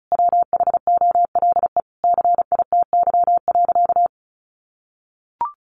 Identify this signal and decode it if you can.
Morse code, heard as a keyed tone, about 34 wpm, 700 Hz. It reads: WHOLE CITY.
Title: Sentences from Top 1400 Words 35wpm